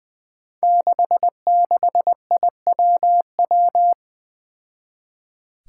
Morse code, heard as a keyed tone, about 20 wpm, 700 Hz. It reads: 66IWW
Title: Five Challenging Letter-Number Combinations 20wpm